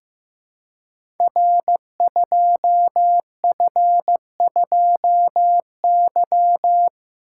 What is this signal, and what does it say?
Morse code, heard as a keyed tone, about 15 wpm, 700 Hz. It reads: R2F2Y
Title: Code Groups - 5 Letter:Numbers - 15wpm